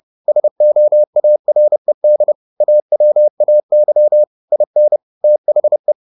Morse code, heard as a keyed tone, about 30 wpm, 600 Hz. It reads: SOARED AWAY IN THE
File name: the old man and the sea / the old man and the sea 1 30wpm